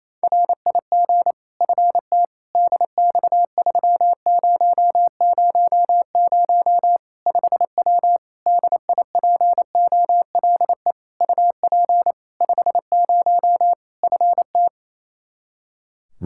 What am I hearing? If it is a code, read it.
Morse code, heard as a keyed tone, about 28 wpm, 700 Hz. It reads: RIG FT DX3000 5W DIPOLE UP 50 FT
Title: Practice QSO - Exchange 2 - Info - 28wpm